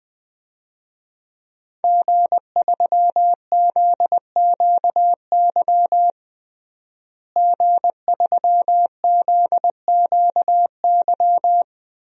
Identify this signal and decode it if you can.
Morse code, heard as a keyed tone, about 19 wpm, 700 Hz. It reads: G3ZQY G3ZQY
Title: DXCC-100 Call Signs - 5 Characters - Repeated 3x - 20wpm